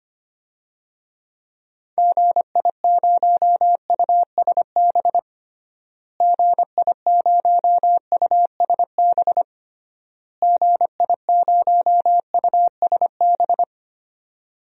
Morse code, heard as a keyed tone, about 25 wpm, 700 Hz. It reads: GI0USB GI0USB GI0USB
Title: DXCC-100 Call Signs - 6 Characters - Repeated 3x - 25wpm